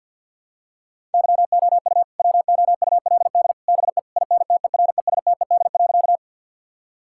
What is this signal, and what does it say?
Morse code, heard as a keyed tone, about 50 wpm, 700 Hz. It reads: YOU WOULD BE INTERESTED.
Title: Sentences from Top 1000 Words - Review Part 6 - 50wpm